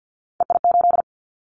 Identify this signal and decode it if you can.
Morse code, heard as a keyed tone, about 50 wpm, 700 Hz. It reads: EI8
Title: Callsign Prefixes for Top 50 DXCC - 50wpm